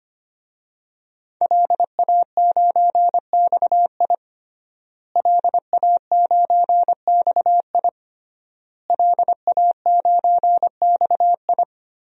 Morse code, heard as a keyed tone, about 25 wpm, 700 Hz. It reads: LA9XI LA9XI LA9XI
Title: DXCC-100 Call Signs - 5 Characters - Repeated 3x - 25wpm